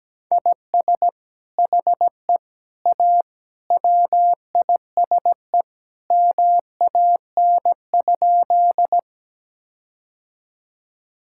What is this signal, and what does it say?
Morse code, heard as a keyed tone, about 17 wpm, 700 Hz. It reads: IS HE A WISE MAN?
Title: Sentences from Top 1600 Words 17wpm